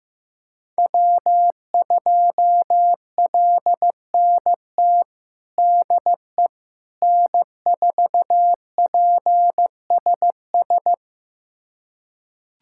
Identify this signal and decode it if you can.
Morse code, heard as a keyed tone, about 15 wpm, 700 Hz. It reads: W2LNT DE N4PSS
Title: USA Callsign DE Callsign - 5 Characters - 15wpm